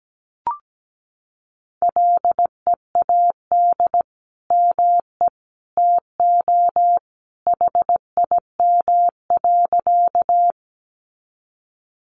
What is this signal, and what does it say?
Morse code, heard as a keyed tone, about 17 wpm, 700 Hz. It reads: LEAD ME TO HIM.
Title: Sentences from Top 500 Words 17wpm